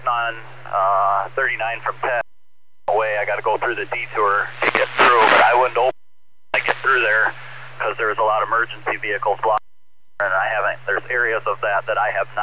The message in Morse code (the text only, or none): none